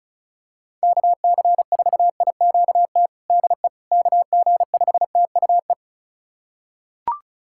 Morse code, heard as a keyed tone, about 36 wpm, 700 Hz. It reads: KC4IQT DE KG5TUE
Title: USA Callsign DE Callsign 35wpm